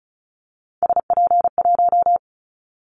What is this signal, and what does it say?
Morse code, heard as a keyed tone, about 34 wpm, 700 Hz. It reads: SP1
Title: Callsign Prefixes for Top 25 DXCC - 35wpm